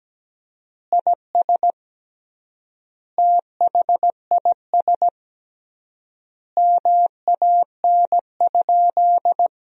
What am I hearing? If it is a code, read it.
Morse code, heard as a keyed tone, about 17 wpm, 700 Hz. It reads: IS THIS MAN?
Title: 3x Word-Spacing - Sentences from Top 200 Words - 17wpm